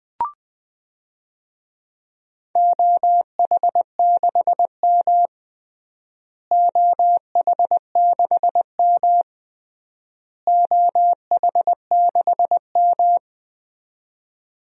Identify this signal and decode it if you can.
Morse code, heard as a keyed tone, about 20 wpm, 700 Hz. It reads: OH6M OH6M OH6M
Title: DXCC-100 Call Signs - 4 Characters - Repeated 3x - 20wpm